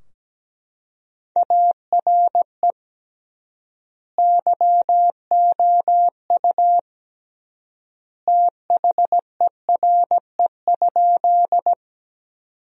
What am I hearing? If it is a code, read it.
Morse code, heard as a keyed tone, about 17 wpm, 700 Hz. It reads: ARE YOU THERE?
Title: 3x Word-Spacing - Sentences from Top 100 Words - 17wpm